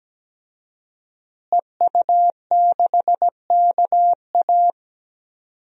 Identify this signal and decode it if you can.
Morse code, heard as a keyed tone, about 17 wpm, 700 Hz. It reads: EU6KA
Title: DXCC-100 Call Signs - 5 Characters - Repeated 2x - 17wpm